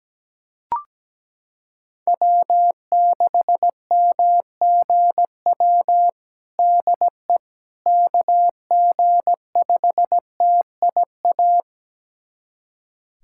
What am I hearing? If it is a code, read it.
Morse code, heard as a keyed tone, about 17 wpm, 700 Hz. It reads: W6MGW DE KG5TIA
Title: USA Callsign DE Callsign 17wpm